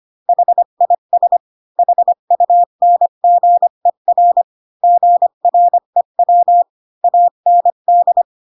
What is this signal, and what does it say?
Morse code, heard as a keyed tone, about 25 wpm, 700 Hz. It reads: HIS HUNGER GREW AND
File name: Pinocchio / Pinocchio ch5 20 25 ik1hge